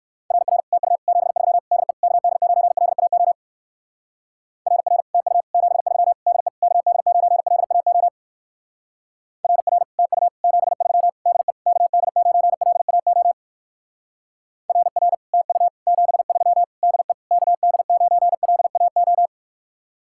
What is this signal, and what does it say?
Morse code, heard as a keyed tone, about 55 wpm, 700 Hz. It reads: RR TU 73 DE KD9LAO RR TU 73 DE KD9LAO RR TU 73 DE KD9LAO RR TU 73 DE KD9LAO
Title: Speed-Racing Common QSO Phrases - 45wpm